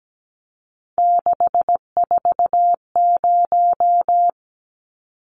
Morse code, heard as a keyed tone, about 17 wpm, 700 Hz. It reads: 640
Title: Numbers - 3-Digit 17wpm